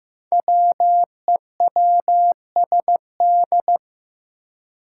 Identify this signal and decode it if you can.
Morse code, heard as a keyed tone, about 15 wpm, 700 Hz. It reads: WEWSD
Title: Five Challenging Letter-Number Combinations 15wpm